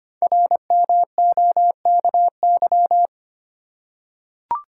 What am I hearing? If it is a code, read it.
Morse code, heard as a keyed tone, about 25 wpm, 700 Hz. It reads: RMOKY
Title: Code Groups - 5 Letters - 25wpm